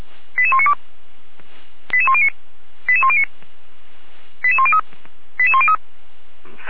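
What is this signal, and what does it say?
Five-tone selective call (ZVEI), heard as quick runs of five tones, tones of about 70 ms. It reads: 88192 88109 88109 88103 88103
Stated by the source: ZVEI five-tone call 88102 received by SDR.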